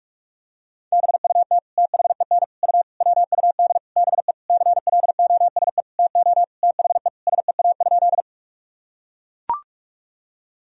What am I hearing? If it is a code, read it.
Morse code, heard as a keyed tone, about 41 wpm, 700 Hz. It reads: BUT THEN U WUD BE CLOSE TO THE SEA?
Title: Sentences from Top 300 Words and Abbreviations 45wpm